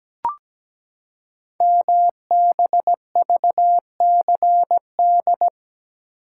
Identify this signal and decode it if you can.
Morse code, heard as a keyed tone, about 17 wpm, 700 Hz. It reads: MBVCD
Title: Code Groups - 5 Letters - 17wpm